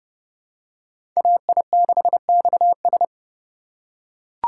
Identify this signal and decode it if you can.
Morse code, heard as a keyed tone, about 29 wpm, 700 Hz. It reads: AI6XS